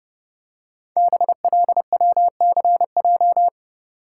Morse code, heard as a keyed tone, about 30 wpm, 700 Hz. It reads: BLWCJ
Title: Code Groups - 5 Letters - 30wpm